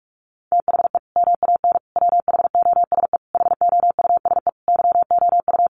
Morse code, heard as a keyed tone, about 44 wpm, 700 Hz. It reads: THE MAN WHOSE HOUSE YOU
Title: Sentences from Top 1200 Words 45wpm